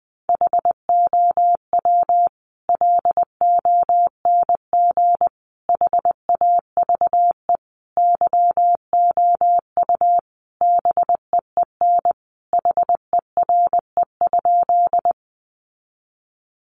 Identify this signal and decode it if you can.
Morse code, heard as a keyed tone, about 20 wpm, 700 Hz. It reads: HOW LONG HAVE YOU BEEN HERE?
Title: Sentences from Top 200 Words 20wpm